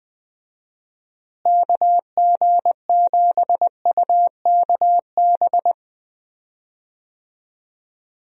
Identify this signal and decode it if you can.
Morse code, heard as a keyed tone, about 21 wpm, 700 Hz. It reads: KG7UKB
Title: USA Call Signs - 6 Characters - Repeated 3x - 20wpm